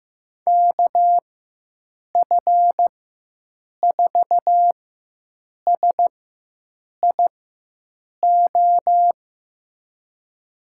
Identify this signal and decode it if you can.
Morse code, heard as a keyed tone, about 15 wpm, spaced as for 7 wpm, 700 Hz. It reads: KF4SIO
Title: USA Call Signs - 6 Characters - 4x Character Spacing - 15wpm